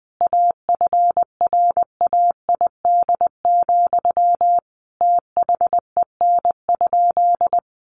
Morse code, heard as a keyed tone, about 20 wpm, 700 Hz. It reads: AFRAID, THEN?
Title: Sentences from Top 1500 Words 20wpm